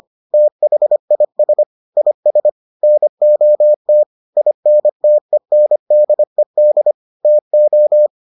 Morse code, heard as a keyed tone, about 25 wpm, 600 Hz. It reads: THIS IS NOT INTENDED TO